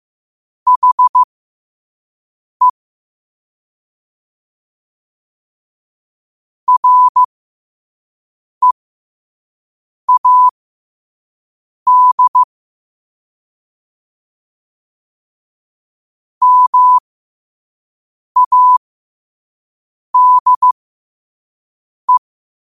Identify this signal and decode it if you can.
Morse code, heard as a keyed tone, about 15 wpm, spaced as for 5 wpm, 1000 Hz. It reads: HE READ MADE
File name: qotdfiles / 5wpm